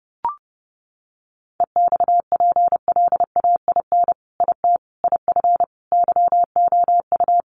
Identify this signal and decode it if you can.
Morse code, heard as a keyed tone, about 30 wpm, 700 Hz. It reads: EXPLAIN IT IF YOU
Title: Sentences from Top 1100 Words 30wpm